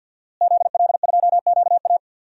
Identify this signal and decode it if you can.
Morse code, heard as a keyed tone, about 46 wpm, 700 Hz. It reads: ZL2QA
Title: DXCC-100 Call Signs - 5 Characters - Repeated 3x - 50wpm